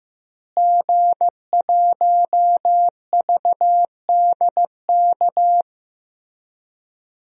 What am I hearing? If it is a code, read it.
Morse code, heard as a keyed tone, about 15 wpm, 700 Hz. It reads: G1VDK
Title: DXCC-100 Call Signs - 5 Characters - Repeated 3x - 15wpm